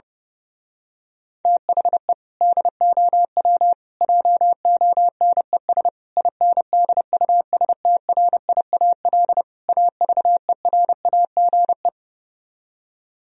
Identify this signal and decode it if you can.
Morse code, heard as a keyed tone, about 28 wpm, 700 Hz. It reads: THE DOW JONES INDUSTRIAL AVERAGE
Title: Speed-Racing - Sets of 5 Words 30wpm